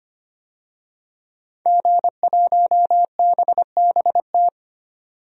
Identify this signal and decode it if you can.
Morse code, heard as a keyed tone, about 25 wpm, 700 Hz. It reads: G1BBT
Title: DXCC-100 Call Signs - 5 Characters - 25wpm